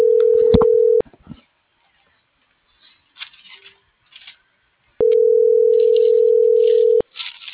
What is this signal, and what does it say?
Ringback tone (North America), 440 + 480 Hz.